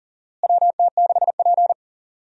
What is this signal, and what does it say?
Morse code, heard as a keyed tone, about 40 wpm, 700 Hz. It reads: WT6P
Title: USA Call Signs - 4 Characters - Repeated 2x - 40wpm